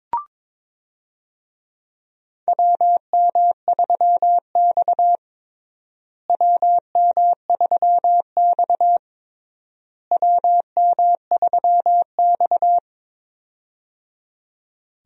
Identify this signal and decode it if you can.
Morse code, heard as a keyed tone, about 22 wpm, 700 Hz. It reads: WM3X WM3X WM3X
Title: USA Call Signs - 4 Characters - Repeated 3x - 22wpm